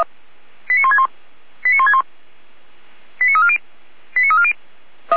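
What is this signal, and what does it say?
Five-tone selective call (ZVEI), heard as quick runs of five tones, tones of about 70 ms. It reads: 89161 89161 89340 89340
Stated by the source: ZVEI five-tone selective call, code 89115, received on an SDR.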